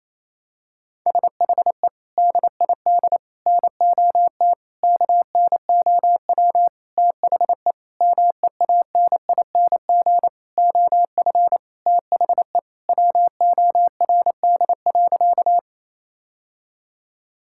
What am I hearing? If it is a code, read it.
Morse code, heard as a keyed tone, about 28 wpm, 700 Hz. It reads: SHE DID NOT KNOW THE MEANING OF THE WORD.
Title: Sentences from Top 1000 Words - Review Part 6 - 28wpm